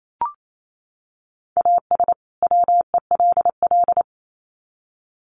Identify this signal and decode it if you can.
Morse code, heard as a keyed tone, about 28 wpm, 700 Hz. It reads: AS WELL
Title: Sets of 2 Words 28wpm